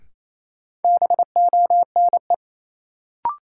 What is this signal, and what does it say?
Morse code, heard as a keyed tone, about 27 wpm, 700 Hz. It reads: BONE
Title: Sentences from Top 1600 Words - New Word Drill - 28wpm